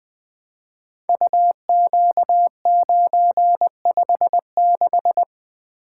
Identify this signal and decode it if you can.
Morse code, heard as a keyed tone, about 20 wpm, 700 Hz. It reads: UQ956
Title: Code Groups - 5 Letter:Numbers - 20wpm